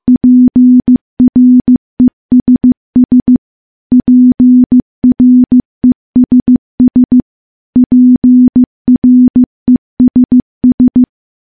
Morse code, heard as a keyed tone, about 15 wpm, 250 Hz. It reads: PRESS PRESS PRESS
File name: morse / morse-training-15WPM